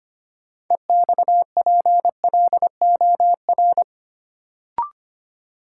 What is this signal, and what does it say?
Morse code, heard as a keyed tone, about 25 wpm, 700 Hz. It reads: EXPLOR
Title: Sentences from Top 1100 Words - New Word Drill - 25wpm